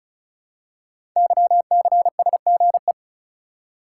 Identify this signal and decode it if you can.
Morse code, heard as a keyed tone, about 36 wpm, 700 Hz. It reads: YCSGE
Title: Code Groups - 5 Letters - 35wpm